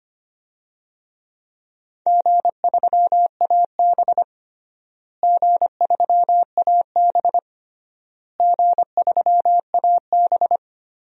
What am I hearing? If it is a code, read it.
Morse code, heard as a keyed tone, about 25 wpm, 700 Hz. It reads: G3AB G3AB G3AB
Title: DXCC-100 Call Signs - 4 Characters - Repeated 3x - 25wpm